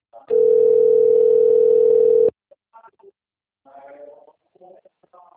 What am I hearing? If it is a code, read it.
Ringback tone (North America), 440 + 480 Hz, with speech.
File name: rg-1600-unknown- - - .3540